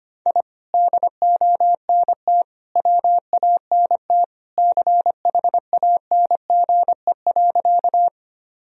Morse code, heard as a keyed tone, about 25 wpm, 700 Hz. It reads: I DONT WANT CHANGE.
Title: Sentences from Top 1000 Words - Review Part 1 - 25wpm